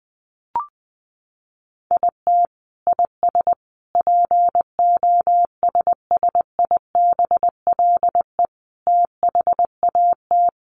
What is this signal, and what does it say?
Morse code, heard as a keyed tone, about 20 wpm, 700 Hz. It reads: IT IS POSSIBLE THAT
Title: Sets of 5 Words Encore2 20wpm